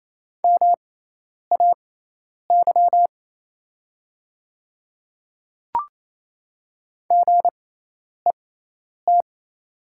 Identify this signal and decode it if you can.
Morse code, heard as a keyed tone, about 27 wpm, spaced as for 10 wpm, 700 Hz. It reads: MAY GET
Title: 6x Character Spacing - Top 300 Words - 28wpm